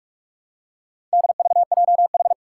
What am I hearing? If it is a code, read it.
Morse code, heard as a keyed tone, about 42 wpm, 700 Hz. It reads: DVJH
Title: Four Challenging Letter-Number Combinations 45wpm